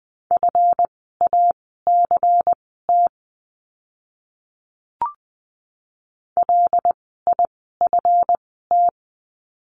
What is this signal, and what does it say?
Morse code, heard as a keyed tone, about 20 wpm, spaced as for 14 wpm, 700 Hz. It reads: FACT LIFT
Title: 139 Lesson 23f - Letter f - Words 20wpm